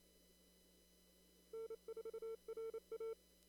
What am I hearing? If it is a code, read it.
Morse code, heard as a keyed tone, about 25 wpm, 410 Hz. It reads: N4RA
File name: CW / kt5j wpx cw-op-k5pi